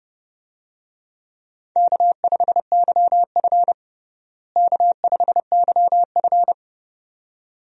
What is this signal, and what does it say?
Morse code, heard as a keyed tone, about 30 wpm, 700 Hz. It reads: K5YF K5YF
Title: USA Call Signs - 4 Characters - Repeated 2x - 30wpm